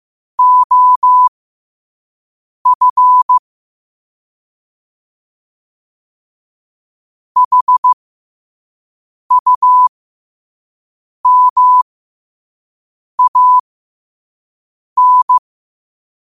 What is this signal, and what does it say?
Morse code, heard as a keyed tone, about 15 wpm, spaced as for 5 wpm, 1000 Hz. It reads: OF HUMAN
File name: 5wpm